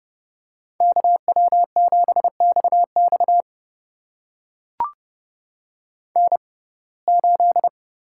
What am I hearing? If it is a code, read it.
Morse code, heard as a keyed tone, about 31 wpm, spaced as for 22 wpm, 700 Hz. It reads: KW7XX N8
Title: USA Call Signs - 5 Characters - 6x Character Spacing - 30wpm